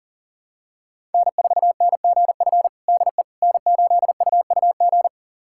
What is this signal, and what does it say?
Morse code, heard as a keyed tone, about 40 wpm, 700 Hz. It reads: N4NGF DE N8UUG
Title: USA Callsign DE Callsign - 5 Characters - 40wpm